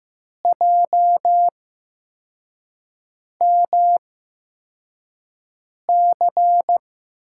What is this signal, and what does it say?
Morse code, heard as a keyed tone, about 15 wpm, spaced as for 4 wpm, 700 Hz. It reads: JMC
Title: USA Call Signs - 5 Characters - 8x Character Spacing - 15wpm